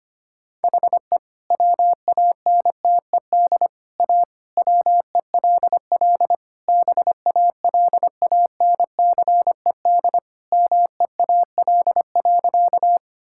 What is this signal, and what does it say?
Morse code, heard as a keyed tone, about 25 wpm, 700 Hz. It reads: HE WANTED A WELL BALANCED MEAL.
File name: practice / 25wpm / All-Age Appropriate Jokes - 25wpm